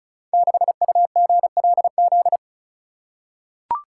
Morse code, heard as a keyed tone, about 36 wpm, 700 Hz. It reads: 6UGLZ
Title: Code Groups - 5 Letter:Numbers - 35wpm